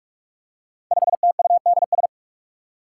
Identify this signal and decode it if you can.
Morse code, heard as a keyed tone, about 44 wpm, 700 Hz. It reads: 5TUDS
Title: Five Challenging Letter-Number Combinations 45wpm